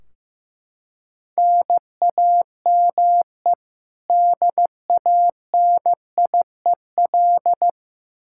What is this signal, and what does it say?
Morse code, heard as a keyed tone, about 16 wpm, 700 Hz. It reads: NAME DANIEL
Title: Practice QSO - Exchange 1 - Beginning - 15wpm